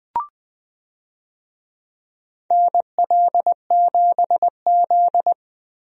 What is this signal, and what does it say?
Morse code, heard as a keyed tone, about 20 wpm, 700 Hz. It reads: NL7Z
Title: USA Call Signs - 4 Characters - 20wpm